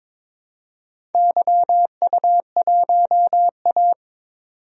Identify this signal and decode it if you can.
Morse code, heard as a keyed tone, about 22 wpm, 700 Hz. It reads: YU1A
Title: DXCC-100 Call Signs - 4 Characters - 22wpm